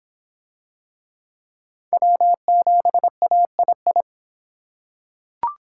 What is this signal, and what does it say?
Morse code, heard as a keyed tone, about 23 wpm, 700 Hz. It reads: W7AII